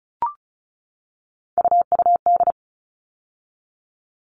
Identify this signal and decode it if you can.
Morse code, heard as a keyed tone, about 34 wpm, 700 Hz. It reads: UUD
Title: All Cut Numbers - 3-Digit 35wpm